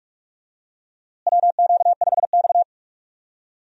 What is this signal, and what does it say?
Morse code, heard as a keyed tone, about 41 wpm, 700 Hz. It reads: WQ5X
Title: USA Call Signs - 4 Characters - Repeated 2x - 45wpm